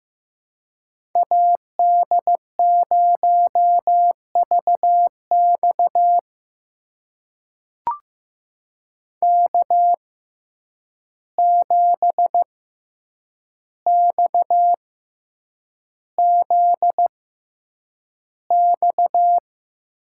Morse code, heard as a keyed tone, about 15 wpm, spaced as for 8 wpm, 700 Hz. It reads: AD0VX K7XZX
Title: USA Call Signs - 5 Characters - 6x Character Spacing - 15wpm